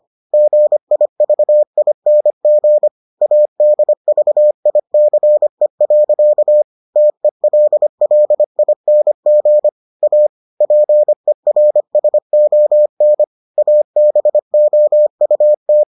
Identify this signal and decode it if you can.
Morse code, heard as a keyed tone, about 25 wpm, 600 Hz. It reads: GIVING ADVICE. TELLING A PERSON ABOUT